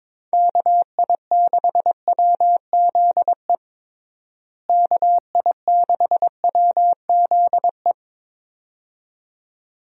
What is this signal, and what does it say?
Morse code, heard as a keyed tone, about 22 wpm, 700 Hz. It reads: KI6WZE KI6WZE
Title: USA Call Signs - 6 Characters - Repeated 2x - 22wpm